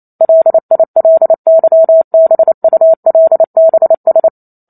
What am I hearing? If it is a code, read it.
Morse code, heard as a keyed tone, about 28 wpm, 650 Hz. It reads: LILYBULBS